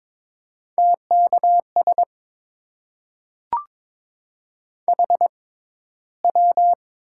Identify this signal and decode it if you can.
Morse code, heard as a keyed tone, about 22 wpm, spaced as for 13 wpm, 700 Hz. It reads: TKS HW
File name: Top 90 QSO Elements - 6x Character Spacing - 22wpm